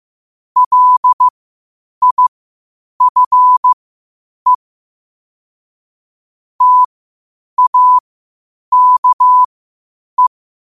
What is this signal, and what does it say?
Morse code, heard as a keyed tone, about 15 wpm, spaced as for 8 wpm, 1000 Hz. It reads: LIFE TAKE